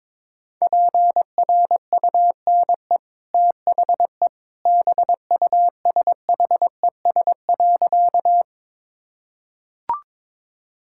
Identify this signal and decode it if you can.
Morse code, heard as a keyed tone, about 22 wpm, 690 Hz. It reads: PRUNE THE BUSHES.
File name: practice / 22wpm / Easy Sentences with Hints - 22wpm